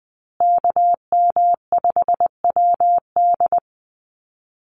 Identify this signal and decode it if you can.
Morse code, heard as a keyed tone, about 20 wpm, 700 Hz. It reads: KM5WD
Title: USA Call Signs Repeated 2x 20wpm